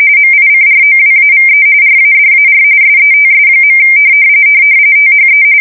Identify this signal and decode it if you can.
RTTY, about 45.45 baud. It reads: MICROWAVE RADAR SETS. THE